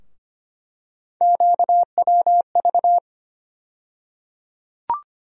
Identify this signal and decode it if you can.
Morse code, heard as a keyed tone, about 25 wpm, 700 Hz. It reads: QWV
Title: USA Call Sign Postfixes - 3 Letters 25wpm